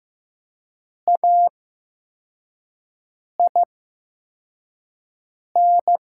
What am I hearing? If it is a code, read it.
Morse code, heard as a keyed tone, about 15 wpm, spaced as for 4 wpm, 700 Hz. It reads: AIN
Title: 8x Character Spacing - Top 500 Words - 15wpm